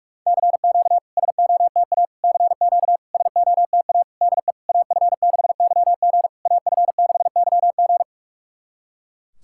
Morse code, heard as a keyed tone, about 46 wpm, 700 Hz. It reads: CQ SOTA CQ SOTA DE AF6YG AF6YG
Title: SOTA QSO Practice 45wpm